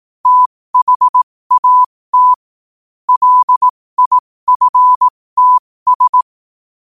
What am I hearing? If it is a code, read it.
Morse code, heard as a keyed tone, about 18 wpm, spaced as for 15 wpm, 1000 Hz. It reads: THAT LIFTS
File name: qotdfiles / 15wpm